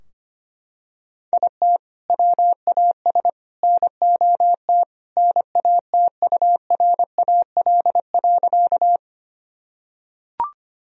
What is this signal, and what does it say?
Morse code, heard as a keyed tone, about 25 wpm, 700 Hz. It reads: IT WAS NOT NATURAL.
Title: Sentences from Top 1000 Words - Review Part 4 - 25wpm